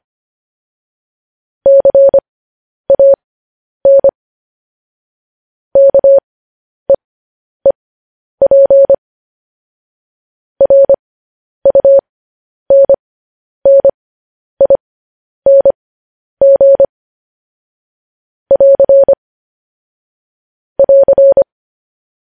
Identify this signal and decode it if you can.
Morse code, heard as a keyed tone, about 25 wpm, spaced as for 10 wpm, 550 Hz. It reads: CAN KEEP RUNNING <AR> <AR>
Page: News Headlines in Morse Code at 10 WPM